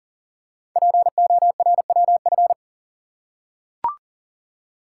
Morse code, heard as a keyed tone, about 38 wpm, 700 Hz. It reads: PORWF
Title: Code Groups - 5 Letters - 40wpm